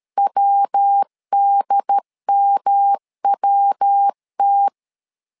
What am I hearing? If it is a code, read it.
Morse code, heard as a keyed tone, about 13 wpm, 800 Hz. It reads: WDMWT
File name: LEZIONI / 3 MWRO / 3 MW 12wpm 1633